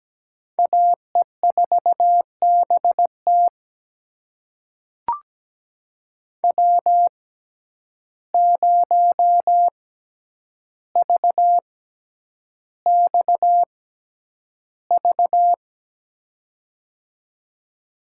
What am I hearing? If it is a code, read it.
Morse code, heard as a keyed tone, about 17 wpm, spaced as for 9 wpm, 700 Hz. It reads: AE4BT W0VXV
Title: USA Call Signs - 5 Characters - 6x Character Spacing - 17wpm